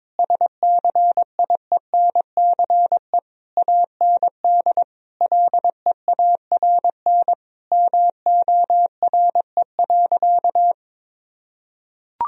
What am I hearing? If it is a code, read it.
Morse code, heard as a keyed tone, about 22 wpm, 700 Hz. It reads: SCIENCE AND LEARN MORE.
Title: Sentences from Top 1000 Words - Review Part 4 - 22wpm